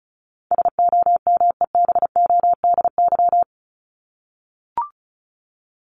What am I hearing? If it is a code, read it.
Morse code, heard as a keyed tone, about 34 wpm, 700 Hz. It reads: SOMEBODY
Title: Top 1751-2000 Words 35wpm